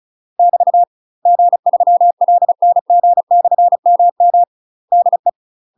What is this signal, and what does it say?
Morse code, heard as a keyed tone, about 36 wpm, 700 Hz. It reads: <BT> G3LNG/MM DE
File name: UK MASTER QSOs TRAINING / 35WPM / UK MASTER QSOs 0018